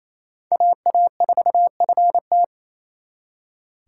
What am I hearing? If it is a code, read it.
Morse code, heard as a keyed tone, about 38 wpm, 700 Hz. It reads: AA4FT